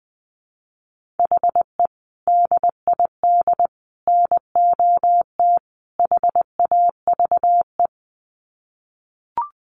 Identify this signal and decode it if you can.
Morse code, heard as a keyed tone, about 20 wpm, 700 Hz. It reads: HE DID NOT HAVE
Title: Sets of 4 Words 20wpm